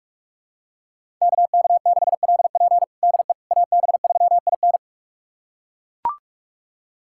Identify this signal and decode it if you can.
Morse code, heard as a keyed tone, about 41 wpm, 700 Hz. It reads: KK6LP DE AB3IN
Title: USA Callsign DE Callsign - 5 Characters - 45wpm